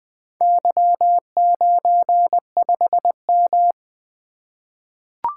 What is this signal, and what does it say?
Morse code, heard as a keyed tone, about 20 wpm, 700 Hz. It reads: Y95M
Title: Four Letter-Number Combinations - 20wpm